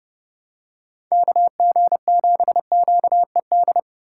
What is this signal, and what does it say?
Morse code, heard as a keyed tone, about 29 wpm, 700 Hz. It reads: KG7QED